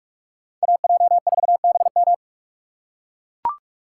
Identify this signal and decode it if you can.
Morse code, heard as a keyed tone, about 43 wpm, 700 Hz. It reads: AJ4BM
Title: USA Call Signs - 5 Characters - 2x Character Spacing - 45wpm